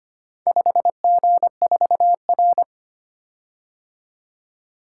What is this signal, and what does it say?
Morse code, heard as a keyed tone, about 25 wpm, 700 Hz. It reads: 5G4R
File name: practice / 25wpm / Four Letter-Number Combinations - 25wpm